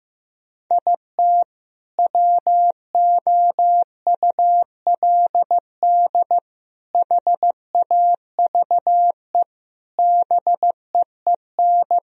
Morse code, heard as a keyed tone, about 15 wpm, 700 Hz. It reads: IT WOULD HAVE BEEN
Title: Sentences from Top 1000 Words - Review Part 4 - 15wpm